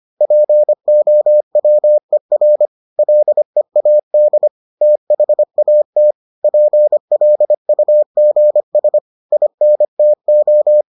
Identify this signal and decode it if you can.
Morse code, heard as a keyed tone, about 25 wpm, 600 Hz. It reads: POWER LEAD THAT PLUGS INTO